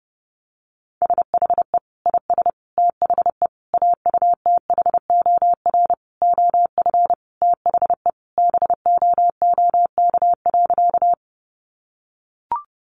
Morse code, heard as a keyed tone, about 30 wpm, 700 Hz. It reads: SHE IS THE AUTHOR OF THE BOOK.
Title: Sentences from Top 1200 Words 30wpm